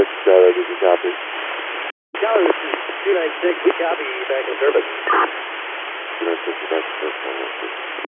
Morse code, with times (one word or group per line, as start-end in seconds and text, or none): none